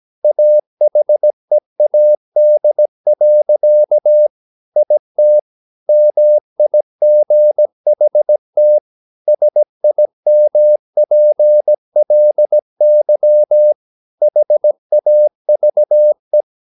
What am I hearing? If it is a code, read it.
Morse code, heard as a keyed tone, about 17 wpm, 600 Hz. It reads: AHEAD. IT MIGHT SIMPLY HAVE
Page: Romanen Never af Ken Follet på engelsk.